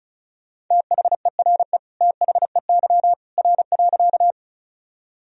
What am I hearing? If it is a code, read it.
Morse code, heard as a keyed tone, about 36 wpm, 700 Hz. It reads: THERE THEY R.
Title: Sentences from Top 100 Words and Abbreviations 35wpm